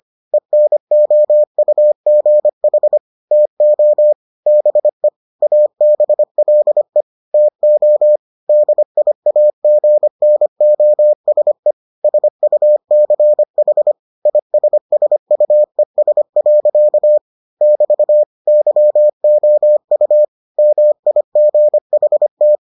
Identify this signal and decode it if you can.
Morse code, heard as a keyed tone, about 25 wpm, 600 Hz. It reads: ENOUGH TO BE ABLE TO DIAGNOSE SUCH ISSUES. <BT> YOU MIGHT